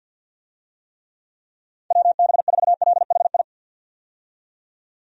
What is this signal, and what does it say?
Morse code, heard as a keyed tone, about 51 wpm, 700 Hz. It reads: WB4LHI
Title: USA Call Signs - 6 Characters - 50wpm